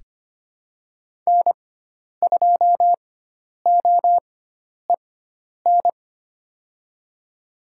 Morse code, heard as a keyed tone, about 25 wpm, spaced as for 10 wpm, 700 Hz. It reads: N2OEN
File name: Lesson 16g - Letter c - Call Signs 25wpm